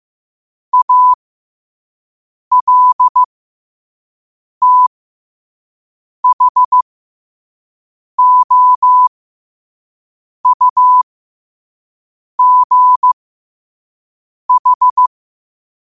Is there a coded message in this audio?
Yes: Morse code from 0.7 to 15.1 s, heard as a keyed tone, about 15 wpm, spaced as for 5 wpm, 1000 Hz. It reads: ALTHOUGH